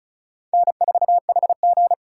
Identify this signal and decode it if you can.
Morse code, heard as a keyed tone, about 34 wpm, 700 Hz. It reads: N4HG
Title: USA Call Signs - 2x Character Spacing - 35wpm